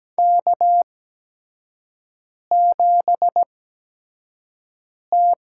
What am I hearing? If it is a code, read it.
Morse code, heard as a keyed tone, about 17 wpm, spaced as for 5 wpm, 700 Hz. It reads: K7T
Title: USA Call Signs - 5 Characters - 8x Character Spacing - 17wpm